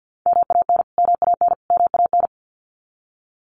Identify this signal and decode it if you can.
Morse code, heard as a keyed tone, about 50 wpm, 700 Hz. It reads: MAN MAN MAN
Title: Top 300 Words Repeated 50wpm